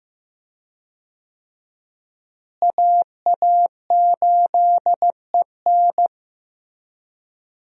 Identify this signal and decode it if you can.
Morse code, heard as a keyed tone, about 15 wpm, 700 Hz. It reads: AA8EN
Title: USA Call Signs - 5 Characters - Repeated 2x - 15wpm